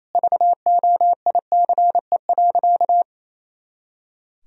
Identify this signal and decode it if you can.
Morse code, heard as a keyed tone, about 28 wpm, 700 Hz. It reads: VOICE.
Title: Sentences from Top 500 Words and Abbreviations 28wpm